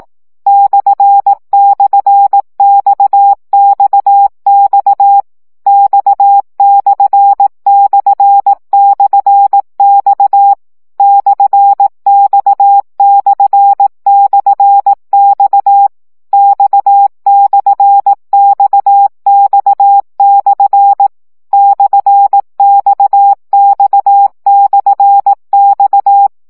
Morse code, heard as a keyed tone, about 18 wpm, 800 Hz. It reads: //XXX X///X /X//X X/XX/ /XX/X
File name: CW gyakorlas x per 15 baud